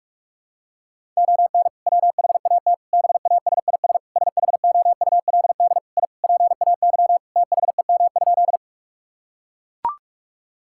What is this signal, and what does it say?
Morse code, heard as a keyed tone, about 42 wpm, 700 Hz. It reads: ON WHAT BASIS SHOULD I PAY THEM?
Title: Sentences from Top 1000 Words - Review Part 5 - 45wpm